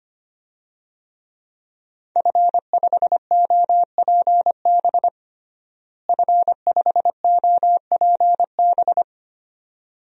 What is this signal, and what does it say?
Morse code, heard as a keyed tone, about 25 wpm, 700 Hz. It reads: F5OPB F5OPB
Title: DXCC-100 Call Signs - 5 Characters - Repeated 2x - 25wpm